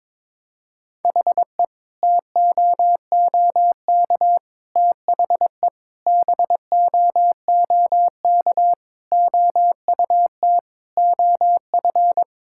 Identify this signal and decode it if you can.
Morse code, heard as a keyed tone, about 22 wpm, 700 Hz. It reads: HE TOOK THE BOOK OUT OF
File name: practice / 22wpm / Sentences with 3-Word Phrases - 22wpm